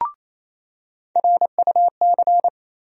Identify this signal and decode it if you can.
Morse code, heard as a keyed tone, about 28 wpm, 700 Hz. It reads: RUC
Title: USA Call Sign Postfixes - 3 Letters 28wpm